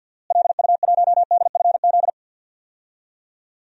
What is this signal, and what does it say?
Morse code, heard as a keyed tone, about 53 wpm, 700 Hz. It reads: LU1DFZ
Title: DXCC-100 Call Signs - 6 Characters - Repeated 2x - 50wpm